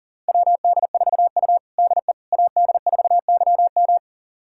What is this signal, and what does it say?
Morse code, heard as a keyed tone, about 40 wpm, 700 Hz. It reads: WD4U DE AD4YM
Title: USA Callsign DE Callsign 40wpm